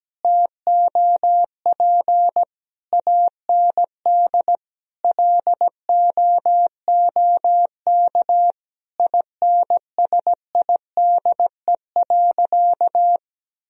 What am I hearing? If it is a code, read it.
Morse code, heard as a keyed tone, about 17 wpm, 700 Hz. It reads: TOP AND LOOK INSIDE.
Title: Sentences from Top 1000 Words - Review Part 2 - 17wpm